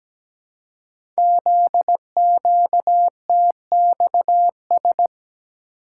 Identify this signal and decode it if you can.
Morse code, heard as a keyed tone, about 17 wpm, 700 Hz. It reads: ZQTXS
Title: Code Groups - 5 Letters - 17wpm